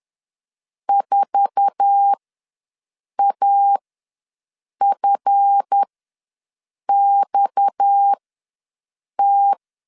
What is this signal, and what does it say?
Morse code, heard as a keyed tone, about 10 wpm, spaced as for 6 wpm, 800 Hz. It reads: 4AFXT
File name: LEZIONI / 8 YJ45 / 8 YJ4 ws 10wpm 4020